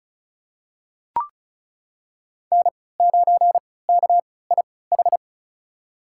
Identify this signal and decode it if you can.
Morse code, heard as a keyed tone, about 36 wpm, spaced as for 20 wpm, 700 Hz. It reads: N9KIH
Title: USA Call Signs - 3x Character Spacing - 35wpm